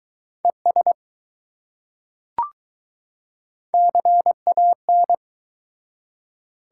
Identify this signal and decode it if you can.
Morse code, heard as a keyed tone, about 20 wpm, 700 Hz. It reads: ES CAN